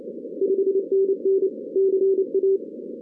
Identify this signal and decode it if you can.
Morse code, heard as a keyed tone, about 28 wpm, 440 Hz. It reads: 5NN CA
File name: 2016 arrl dx cw / 2016-ARRL-DX-CW-K5TR- - 0-41